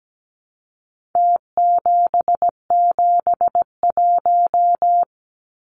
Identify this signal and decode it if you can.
Morse code, heard as a keyed tone, about 17 wpm, 700 Hz. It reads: T771
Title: Numbers - Contest Style - 4-Digit 17wpm